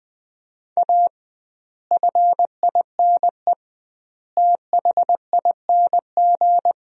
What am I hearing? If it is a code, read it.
Morse code, heard as a keyed tone, about 20 wpm, 700 Hz. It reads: A FINE THING
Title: 2x Word-Spacing - Sentences from Top 500 Words - 20wpm